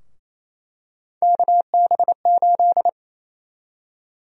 Morse code, heard as a keyed tone, about 28 wpm, 700 Hz. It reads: KB8
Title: USA Call Sign Prefixes - Repeated 2x - 28wpm